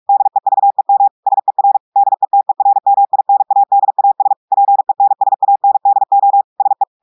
Morse code, heard as a keyed tone, about 46 wpm, 800 Hz. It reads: DEVEM SER DETERMINADAS PENSANDO SE
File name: audios / portugais / portugais texte en clair 2 vitesse 45 mots minutes